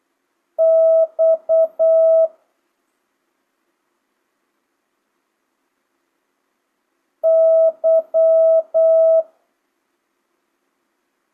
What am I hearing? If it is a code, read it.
Morse code, heard as a keyed tone, about 8 wpm, 650 Hz. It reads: X Y
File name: jh8cbh / 2025cw